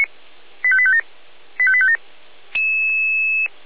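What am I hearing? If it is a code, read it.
Five-tone selective call (ZVEI), heard as quick runs of five tones, tones of about 70 ms. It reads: 85857 85857
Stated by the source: ZVEI five-tone selective call received on SDR.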